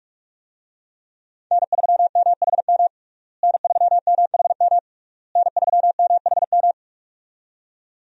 Morse code, heard as a keyed tone, about 46 wpm, 700 Hz. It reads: N3MHM N3MHM N3MHM
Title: USA Call Signs - 5 Characters - Repeated 3x - 45wpm